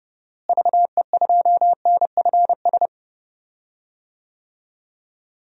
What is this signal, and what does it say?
Morse code, heard as a keyed tone, about 31 wpm, 700 Hz. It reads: VE2NFS
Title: DXCC-100 Call Signs - 6 Characters - Repeated 2x - 30wpm